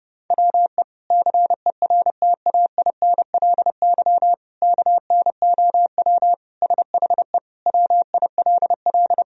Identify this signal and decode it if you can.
Morse code, heard as a keyed tone, about 30 wpm, 700 Hz. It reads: WE CERTAINLY KNOW SHE WILL
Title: Sentences from Top 1000 Words - Review Part 4 - 30wpm